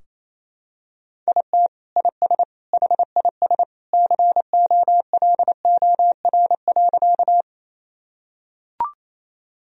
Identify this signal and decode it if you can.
Morse code, heard as a keyed tone, about 28 wpm, 700 Hz. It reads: IT IS HIS COLOR.
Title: Sentences from Top 1000 Words - Review Part 2 - 28wpm